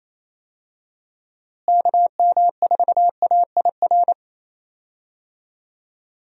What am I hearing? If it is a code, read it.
Morse code, heard as a keyed tone, about 27 wpm, 700 Hz. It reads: KM4AIR
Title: USA Call Signs - 6 Characters - Repeated 2x - 28wpm